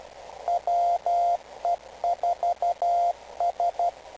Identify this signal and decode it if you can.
Morse code, heard as a keyed tone, about 12 wpm, 700 Hz, with radio noise. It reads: WE4S
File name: kl7fdq / beacons / we4s-rst586